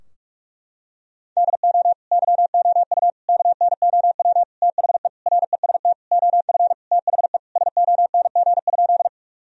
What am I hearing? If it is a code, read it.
Morse code, heard as a keyed tone, about 46 wpm, 700 Hz. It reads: DO YOU KNOW THE REST OF THE SONG?
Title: Sentences from Top 1000 Words - Review Part 2 - 45wpm